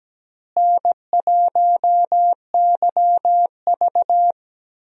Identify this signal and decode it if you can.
Morse code, heard as a keyed tone, about 17 wpm, 700 Hz. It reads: N1YV
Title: USA Call Signs - 4 Characters - Repeated 2x - 17wpm